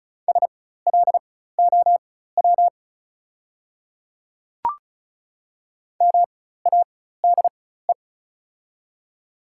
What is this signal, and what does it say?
Morse code, heard as a keyed tone, about 36 wpm, spaced as for 16 wpm, 690 Hz. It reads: SLOW MADE
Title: 4x Character Spacing - Top 500 Words - 35wpm